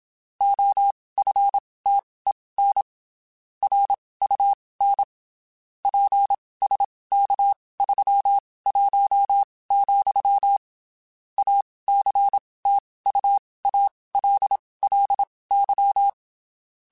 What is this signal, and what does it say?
Morse code, heard as a keyed tone, about 27 wpm, spaced as for 19 wpm, 800 Hz. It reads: OFTEN RUN PSK31, ACTUALLY
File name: (spaced) / QSOs Spaced-mp3s / kc7haa @25 sp